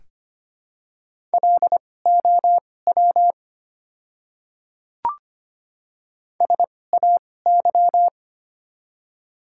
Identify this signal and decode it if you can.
Morse code, heard as a keyed tone, about 25 wpm, spaced as for 18 wpm, 700 Hz. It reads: LOW SAY
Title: 2x Character Spacing - Top 200 Words - 25wpm